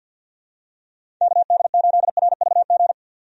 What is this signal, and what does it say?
Morse code, heard as a keyed tone, about 53 wpm, 700 Hz. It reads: KD8RVG
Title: USA Call Signs - 6 Characters - 50wpm